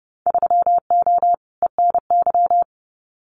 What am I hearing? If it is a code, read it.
Morse code, heard as a keyed tone, about 30 wpm, 700 Hz. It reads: 3O ENY
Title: Winter Field Day Exchanges Encore 30wpm